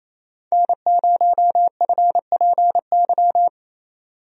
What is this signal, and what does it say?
Morse code, heard as a keyed tone, about 27 wpm, 700 Hz. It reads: N0FPY
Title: USA Call Signs - 5 Characters - 28wpm